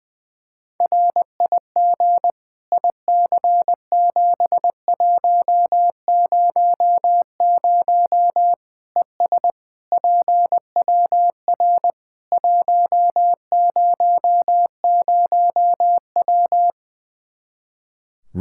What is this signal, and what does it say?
Morse code, heard as a keyed tone, about 21 wpm, 700 Hz. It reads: RIG IC7100 ES PWR 100W
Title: Practice QSO - Exchange 2 - Info - 20wpm